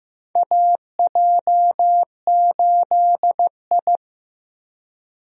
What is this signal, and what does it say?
Morse code, heard as a keyed tone, about 15 wpm, 700 Hz. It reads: AJ8I
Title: USA Call Signs - 4 Characters - Repeated 2x - 15wpm